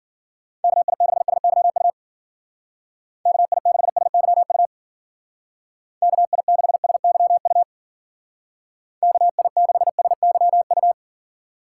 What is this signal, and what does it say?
Morse code, heard as a keyed tone, about 50 wpm, 700 Hz. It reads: KI6SYU KI6SYU KI6SYU KI6SYU